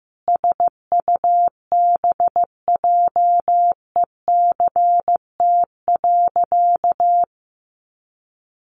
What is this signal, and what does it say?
Morse code, heard as a keyed tone, about 15 wpm, 700 Hz. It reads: SUBJECT.
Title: Sentences from Top 1500 Words 15wpm